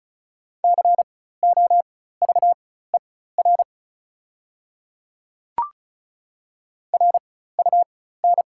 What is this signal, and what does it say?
Morse code, heard as a keyed tone, about 36 wpm, spaced as for 16 wpm, 690 Hz. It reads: COVER RUN
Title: 4x Character Spacing - Top 300 Words - 35wpm